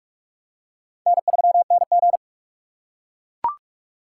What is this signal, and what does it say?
Morse code, heard as a keyed tone, about 43 wpm, 700 Hz. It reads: N3NG
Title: USA Call Signs - 4 Characters - 2x Character Spacing - 45wpm